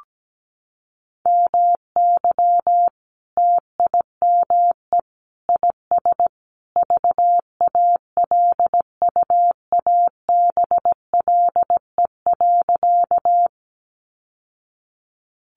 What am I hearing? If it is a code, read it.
Morse code, heard as a keyed tone, about 17 wpm, 700 Hz. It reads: MY TIME IS VALUABLE.
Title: Sentences from Top 1100 Words 17wpm